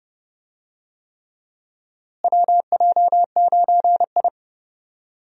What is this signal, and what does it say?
Morse code, heard as a keyed tone, about 30 wpm, 700 Hz. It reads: WJ9I
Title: USA Call Signs - 4 Characters - Repeated 3x - 30wpm